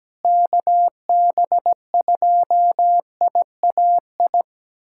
Morse code, heard as a keyed tone, about 17 wpm, 700 Hz. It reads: KB2IAI
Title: USA Callsign DE Callsign - 6 Characters - 17wpm